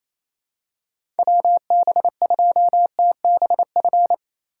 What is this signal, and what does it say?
Morse code, heard as a keyed tone, about 27 wpm, 700 Hz. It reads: WB2TBF
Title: USA Call Signs - 3x Character Spacing - 28wpm